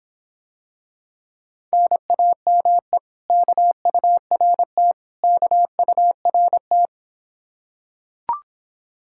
Morse code, heard as a keyed tone, about 26 wpm, 700 Hz. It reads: NAME KURT KURT